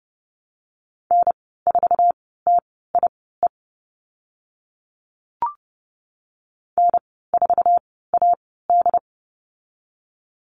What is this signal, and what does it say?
Morse code, heard as a keyed tone, about 30 wpm, spaced as for 17 wpm, 700 Hz. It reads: N4TIE N4AD
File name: Lesson 14g - Number 2 - Call Signs 30wpm